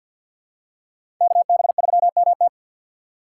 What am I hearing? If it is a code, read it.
Morse code, heard as a keyed tone, about 51 wpm, 700 Hz. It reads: KB3MT
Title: USA Call Signs - 3x Character Spacing - 50wpm